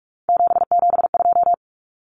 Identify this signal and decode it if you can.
Morse code, heard as a keyed tone, about 44 wpm, 700 Hz. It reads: 772
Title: Numbers - 3-Digit 45wpm